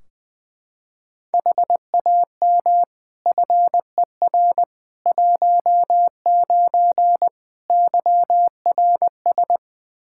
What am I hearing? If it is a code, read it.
Morse code, heard as a keyed tone, about 20 wpm, 700 Hz. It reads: HAM FER 19 YRS
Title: K1USN SST QSO Practice 20wpm